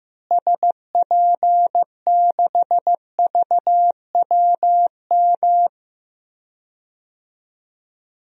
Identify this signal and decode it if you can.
Morse code, heard as a keyed tone, about 15 wpm, 700 Hz. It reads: SP6VWM
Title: DXCC-100 Call Signs - 6 Characters - Repeated 2x - 15wpm